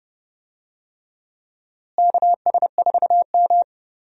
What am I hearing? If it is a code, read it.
Morse code, heard as a keyed tone, about 29 wpm, 700 Hz. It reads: KS4M
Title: USA Call Signs - 4 Characters - Repeated 3x - 30wpm